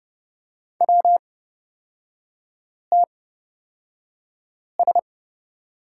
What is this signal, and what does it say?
Morse code, heard as a keyed tone, about 29 wpm, spaced as for 5 wpm, 700 Hz. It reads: WTS
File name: Lesson 37e - Letter x - Three Characters 30wpm